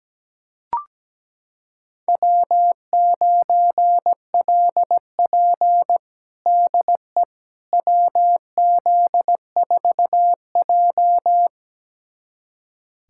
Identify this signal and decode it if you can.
Morse code, heard as a keyed tone, about 17 wpm, 700 Hz. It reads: W9LP DE WZ4J
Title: USA Callsign DE Callsign - 4 Characters - 17wpm